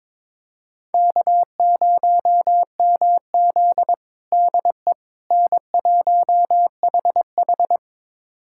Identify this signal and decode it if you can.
Morse code, heard as a keyed tone, about 21 wpm, 700 Hz. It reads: K0MZ DE N1HH
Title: USA Callsign DE Callsign - 4 Characters - 22wpm